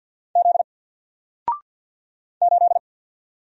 Morse code, heard as a keyed tone, about 50 wpm, 700 Hz. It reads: Z 8
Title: Single Letter-Number 50wpm